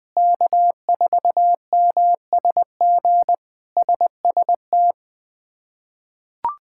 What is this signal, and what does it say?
Morse code, heard as a keyed tone, about 20 wpm, 700 Hz. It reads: K4MSG SST
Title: K1USN SST QSO Practice - 20wpm